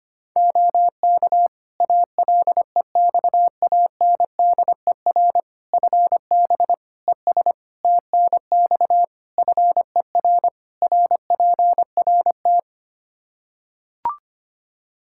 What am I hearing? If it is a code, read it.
Morse code, heard as a keyed tone, about 25 wpm, 700 Hz. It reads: OK ALEXANDER FB ES TNX FER RPRT
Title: K1USN SST QSO Practice 25wpm